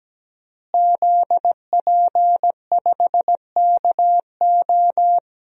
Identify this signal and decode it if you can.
Morse code, heard as a keyed tone, about 17 wpm, 700 Hz. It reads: ZP5KO
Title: DXCC-100 Call Signs - 5 Characters - Repeated 3x - 17wpm